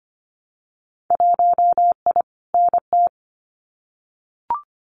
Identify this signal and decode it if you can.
Morse code, heard as a keyed tone, about 25 wpm, 700 Hz. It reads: 1I NT
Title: Winter Field Day Exchanges Encore 25wpm